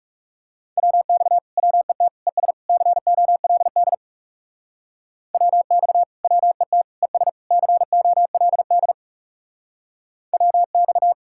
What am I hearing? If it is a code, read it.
Morse code, heard as a keyed tone, about 36 wpm, 700 Hz. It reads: WX WET ES COLD WX WET ES COLD WX
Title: Speed-Racing Common QSO Phrases - 30wpm